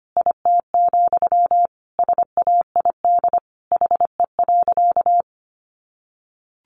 Morse code, heard as a keyed tone, about 25 wpm, 700 Hz. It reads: IT, SAID HE.
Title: Sentences from Top 1100 Words 25wpm